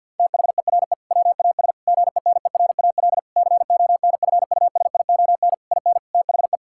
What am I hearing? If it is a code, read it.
Morse code, heard as a keyed tone, about 50 wpm, 700 Hz. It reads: THERE WAS GENERAL CONFUSION IN THE
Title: Sentences from Top 1600 Words 50wpm